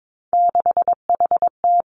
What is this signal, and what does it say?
Morse code, heard as a keyed tone, about 22 wpm, 700 Hz. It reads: 6HT
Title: Three Challenging Letter-Number Combinations 22wpm